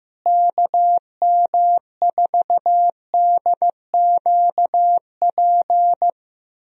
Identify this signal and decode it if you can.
Morse code, heard as a keyed tone, about 15 wpm, 700 Hz. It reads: KM4DQP
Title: USA Call Signs - 6 Characters - Repeated 3x - 15wpm